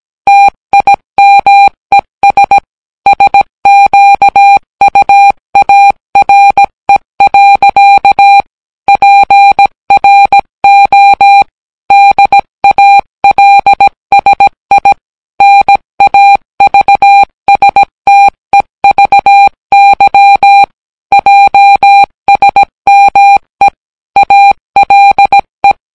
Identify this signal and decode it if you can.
Morse code, heard as a keyed tone, about 17 wpm, 800 Hz. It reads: TIMES SQUARE. PRO DALSI NAVSTEVY JSME ALE